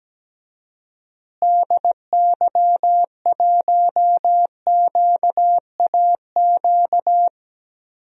Morse code, heard as a keyed tone, about 17 wpm, 700 Hz. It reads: DY1QAQ
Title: DXCC-100 Call Signs - 6 Characters - Repeated 2x - 17wpm